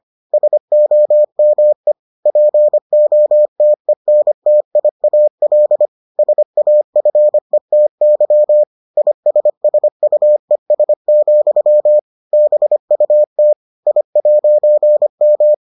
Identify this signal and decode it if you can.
Morse code, heard as a keyed tone, about 25 wpm, 600 Hz. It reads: SOME POTENTIAL SAFETY ISSUES, BUT I'M